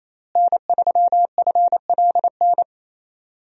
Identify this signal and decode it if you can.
Morse code, heard as a keyed tone, about 28 wpm, 700 Hz. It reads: N3FLN
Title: USA Call Signs - 5 Characters - Repeated 2x - 28wpm